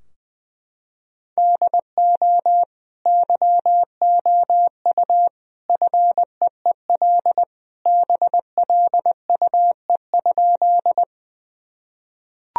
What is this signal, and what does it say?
Morse code, heard as a keyed tone, about 20 wpm, 700 Hz. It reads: DO YOU FEEL BLUE?
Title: Sentences from Top 1000 Words - Review Part 3 - 20wpm